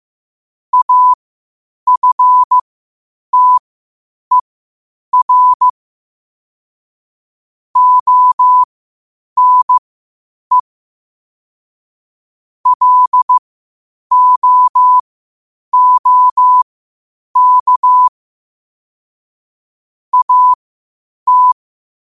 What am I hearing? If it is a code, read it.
Morse code, heard as a keyed tone, about 15 wpm, spaced as for 8 wpm, 1000 Hz. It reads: AFTER ONE LOOK AT
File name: qotdfiles / 8wpm